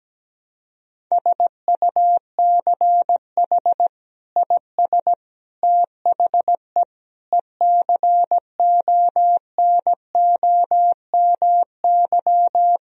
Morse code, heard as a keyed tone, about 17 wpm, 700 Hz. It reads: SUCH IS THE ECONOMY
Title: Sentences from Top 1000 Words - Review Part 3 - 17wpm